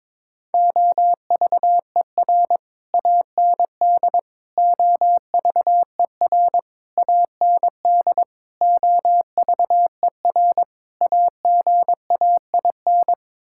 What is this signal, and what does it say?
Morse code, heard as a keyed tone, about 22 wpm, 700 Hz. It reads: OVER AND OVER AND OVER AGAIN
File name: practice / 22wpm / Sets of 6 Words - 22wpm